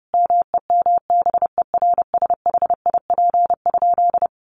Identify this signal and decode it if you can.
Morse code, heard as a keyed tone, about 30 wpm, 700 Hz. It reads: MEMBERSHIP?
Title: Sentences from Top 1400 Words 30wpm